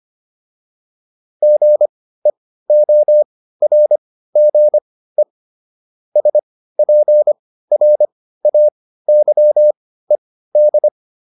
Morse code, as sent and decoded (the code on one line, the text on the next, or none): --. --- .-. --. / ... .--. .-. .- -.-- . -..
GEORGE SPRAYED